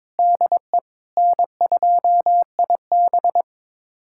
Morse code, heard as a keyed tone, about 22 wpm, 700 Hz. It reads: DE N2IB
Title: USA Callsign DE Callsign - 4 Characters - 22wpm